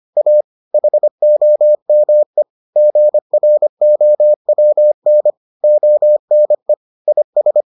Morse code, heard as a keyed tone, about 25 wpm, 600 Hz. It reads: A HOME GROWN ONE IS